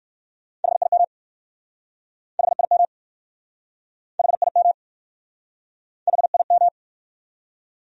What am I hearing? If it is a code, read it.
Morse code, heard as a keyed tone, about 59 wpm, 700 Hz. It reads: HIM HIM HIM HIM